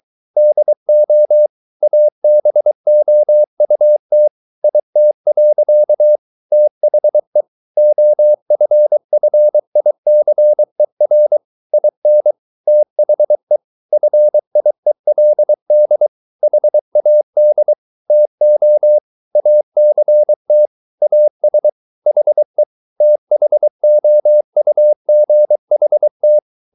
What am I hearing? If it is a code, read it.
Morse code, heard as a keyed tone, about 23 wpm, 600 Hz. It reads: DO ABOUT IT. THE OFFICER IN THE FIELD HAD TO ACT AS HE THOUGHT